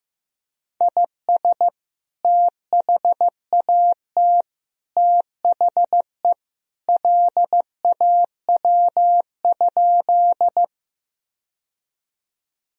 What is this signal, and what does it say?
Morse code, heard as a keyed tone, about 15 wpm, 700 Hz. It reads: IS THAT THE LAW?
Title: Sentences from Top 1000 Words - Review Part 3 - 15wpm